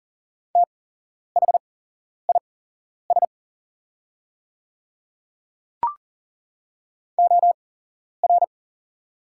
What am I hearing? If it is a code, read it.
Morse code, heard as a keyed tone, about 40 wpm, spaced as for 11 wpm, 700 Hz. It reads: THIS OR
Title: 8x Character Spacing - Top 100 Words - 40wpm